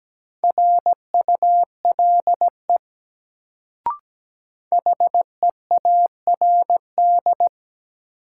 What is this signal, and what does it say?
Morse code, heard as a keyed tone, about 17 wpm, 700 Hz. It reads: RULE HEARD
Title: Sentences from Top 400 Words - New Word Drill - 17wpm